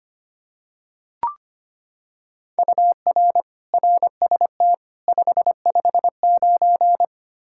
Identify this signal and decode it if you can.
Morse code, heard as a keyed tone, about 25 wpm, 700 Hz. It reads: UR RST 559
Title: Practice QSO - Exchange 1 - Beginning - 25wpm